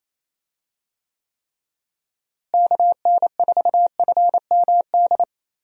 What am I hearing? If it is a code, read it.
Morse code, heard as a keyed tone, about 27 wpm, 700 Hz. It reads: KN4FMD
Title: USA Call Signs - 6 Characters - Repeated 3x - 28wpm